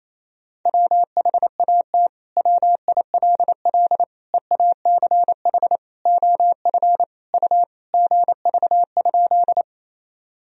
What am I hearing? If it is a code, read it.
Morse code, heard as a keyed tone, about 28 wpm, 700 Hz. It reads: WHAT WILL EACH OF U GV?
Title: Sentences from Top 200 Words and Abbreviations 28wpm